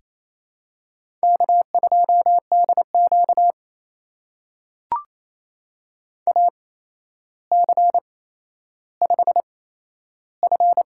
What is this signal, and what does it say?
Morse code, heard as a keyed tone, about 28 wpm, spaced as for 12 wpm, 700 Hz. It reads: K2DQ AC5F
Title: USA Call Signs - 4 Characters - 8x Character Spacing - 28wpm